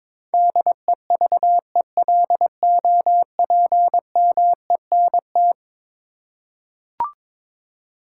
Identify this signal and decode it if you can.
Morse code, heard as a keyed tone, about 22 wpm, 700 Hz. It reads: DEVELOPMENT
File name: practice / 22wpm / Eleven Letter Words - 22wpm